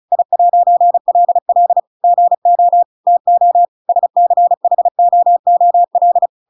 Morse code, heard as a keyed tone, about 34 wpm, 700 Hz. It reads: I'LL GO TO SCHOOL
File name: Pinocchio / Pinocchio ch8 35 35 ik1hge